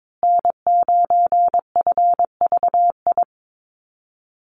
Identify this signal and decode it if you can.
Morse code, heard as a keyed tone, about 22 wpm, 700 Hz. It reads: N9FVI
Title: USA Call Signs Repeated 3x 22wpm